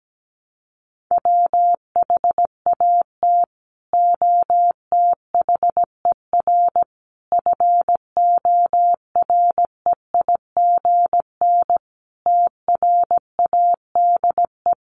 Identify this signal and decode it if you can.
Morse code, heard as a keyed tone, about 17 wpm, 700 Hz. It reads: WHAT OTHER FOREIGN TRADE
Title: Sentences from Top 1400 Words 17wpm